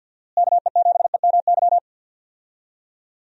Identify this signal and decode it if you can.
Morse code, heard as a keyed tone, about 51 wpm, 700 Hz. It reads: KE7EMY
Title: USA Call Signs - 6 Characters - Repeated 3x - 50wpm